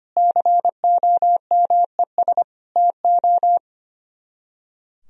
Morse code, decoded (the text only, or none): COMES TO